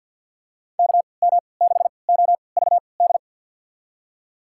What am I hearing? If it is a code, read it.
Morse code, heard as a keyed tone, about 50 wpm, spaced as for 28 wpm, 700 Hz. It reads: KM6OVD
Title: USA Call Signs - 6 Characters - 3x Character Spacing - 50wpm